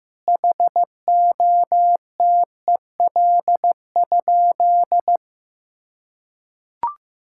Morse code, heard as a keyed tone, about 15 wpm, 700 Hz. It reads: HOTEL?
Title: Sentences from Top 1600 Words 15wpm